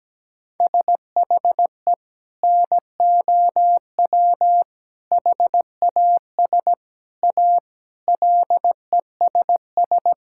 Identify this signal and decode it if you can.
Morse code, heard as a keyed tone, about 17 wpm, 700 Hz. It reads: SHE NOW HAS A LESS
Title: Sentences from Top 1000 Words - Review Part 6 - 17wpm